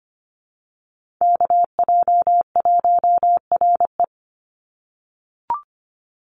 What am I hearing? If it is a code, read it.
Morse code, heard as a keyed tone, about 25 wpm, 700 Hz. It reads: KJ1RE
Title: Call Signs - 25wpm